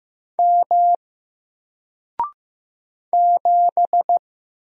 Morse code, heard as a keyed tone, about 15 wpm, 700 Hz. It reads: M 7
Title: Single Letter-Number Alt 15wpm